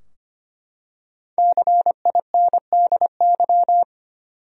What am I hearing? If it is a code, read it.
Morse code, heard as a keyed tone, about 25 wpm, 700 Hz. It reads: CINDY
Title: USA Common Names - 25wpm